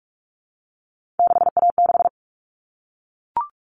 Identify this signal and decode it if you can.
Morse code, heard as a keyed tone, about 44 wpm, 700 Hz. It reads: 6A6
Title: All Cut Numbers - 3-Digit 45wpm